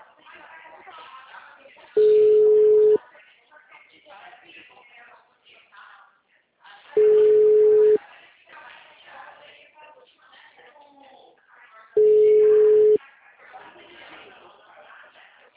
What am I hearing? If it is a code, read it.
Ringback tone (Europe, 425 Hz), with speech.